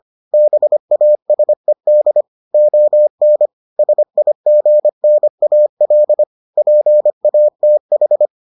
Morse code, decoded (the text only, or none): BASED ON SIGNAL PATH